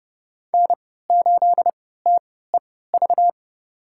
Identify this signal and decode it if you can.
Morse code, heard as a keyed tone, about 30 wpm, spaced as for 17 wpm, 700 Hz. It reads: N8TEV
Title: USA Call Signs - 3x Character Spacing - 30wpm